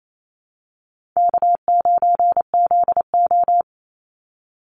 Morse code, heard as a keyed tone, about 28 wpm, 700 Hz. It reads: K9ZO
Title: Easy Call Signs - 28wpm